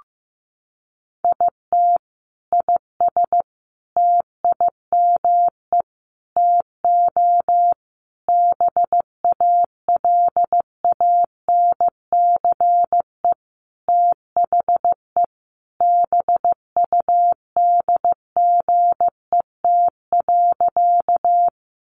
Morse code, heard as a keyed tone, about 15 wpm, 700 Hz. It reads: IT IS TIME TO BALANCE THE BUDGET.
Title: Sentences from Top 1200 Words 15wpm